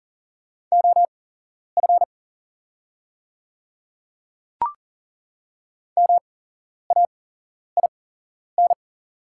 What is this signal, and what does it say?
Morse code, heard as a keyed tone, about 40 wpm, spaced as for 11 wpm, 690 Hz. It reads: OF MAIN
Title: 8x Character Spacing - Top 300 Words - 40wpm